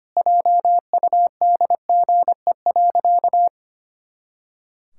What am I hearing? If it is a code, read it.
Morse code, heard as a keyed tone, about 25 wpm, 700 Hz. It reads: JUDGE.
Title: Pangram Challenge v2 - 25wpm